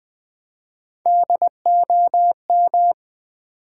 Morse code, heard as a keyed tone, about 20 wpm, 700 Hz. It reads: DOM
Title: Build A Word - 3 Letters At A Time - 9-Letter Words - 20wpm